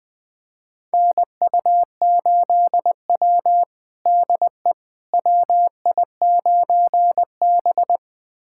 Morse code, heard as a keyed tone, about 20 wpm, 700 Hz. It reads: NU8W DE WI9B
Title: USA Callsign DE Callsign - 4 Characters - 20wpm